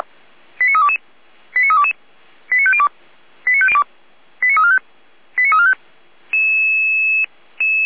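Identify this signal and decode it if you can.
Five-tone selective call (ZVEI), heard as quick runs of five tones, tones of about 70 ms. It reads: 89322 89322 89502 89345 89345
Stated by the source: ZVEI five-tone alert received by SDR.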